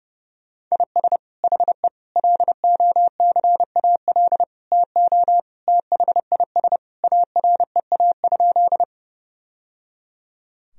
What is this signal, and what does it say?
Morse code, heard as a keyed tone, about 30 wpm, 700 Hz. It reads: IS HE LOCAL TO THIS AREA?
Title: Sentences from Top 1000 Words - Review Part 3 - 30wpm